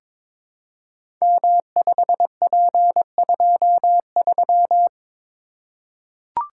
Morse code, heard as a keyed tone, about 22 wpm, 700 Hz. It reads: M5P23
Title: Code Groups - 5 Letter:Numbers - 22wpm